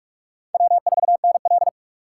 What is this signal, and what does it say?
Morse code, heard as a keyed tone, about 42 wpm, 700 Hz. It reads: W4NL
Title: USA Call Signs - 6x Character Spacing - 45wpm